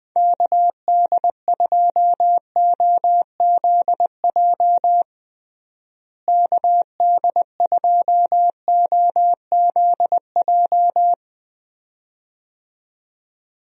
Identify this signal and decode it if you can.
Morse code, heard as a keyed tone, about 20 wpm, 700 Hz. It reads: KD2OZJ KD2OZJ
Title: USA Call Signs - 6 Characters - Repeated 2x - 20wpm